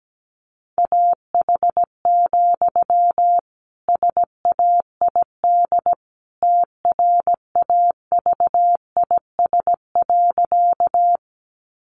Morse code, heard as a keyed tone, about 17 wpm, 700 Hz. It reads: AH, SAID TRAVIS.
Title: A Sound Of Thunder 17wpm